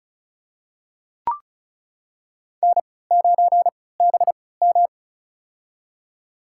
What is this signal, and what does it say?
Morse code, heard as a keyed tone, about 34 wpm, spaced as for 20 wpm, 700 Hz. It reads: N9BM
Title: USA Call Signs - 4 Characters - 3x Character Spacing - 35wpm